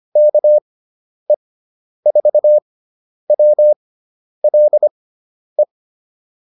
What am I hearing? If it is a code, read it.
Morse code, heard as a keyed tone, about 25 wpm, spaced as for 10 wpm, 600 Hz. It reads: KE4WLE